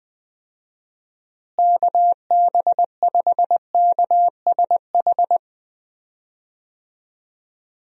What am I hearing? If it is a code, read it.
Morse code, heard as a keyed tone, about 20 wpm, 700 Hz. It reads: KB5KSH
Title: USA Call Signs - 6 Characters - Repeated 2x - 20wpm